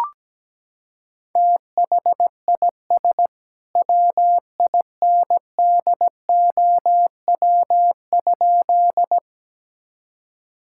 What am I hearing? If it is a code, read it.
Morse code, heard as a keyed tone, about 17 wpm, 700 Hz. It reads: THIS WINDOW?
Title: Sentences from Top 1600 Words 17wpm